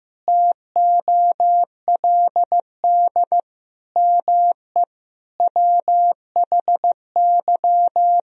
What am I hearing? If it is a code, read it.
Morse code, heard as a keyed tone, about 15 wpm, 700 Hz. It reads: TOLD ME WHY
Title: Sentences from Top 1000 Words - Review Part 2 - 15wpm